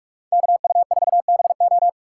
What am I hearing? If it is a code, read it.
Morse code, heard as a keyed tone, about 46 wpm, 700 Hz. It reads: KU4BO
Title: USA Call Signs - 5 Characters - 4x Character Spacing - 45wpm